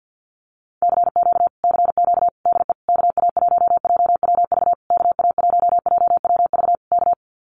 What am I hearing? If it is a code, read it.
Morse code, heard as a keyed tone, about 48 wpm, 700 Hz. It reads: CQ CQ DE KA1JWV KA1JWV K
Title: Common QSO Phrases Encore 50wpm